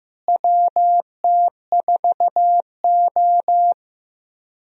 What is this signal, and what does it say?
Morse code, heard as a keyed tone, about 15 wpm, 700 Hz. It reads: WT4O
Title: USA Call Signs - 4 Characters - Repeated 2x - 15wpm